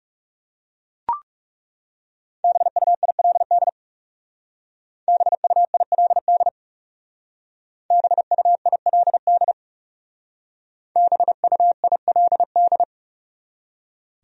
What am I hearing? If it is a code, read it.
Morse code, heard as a keyed tone, about 37 wpm, 700 Hz. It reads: BUILD BUILD BUILD BUILD
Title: Speed-Racing Top 200 Words 30wpm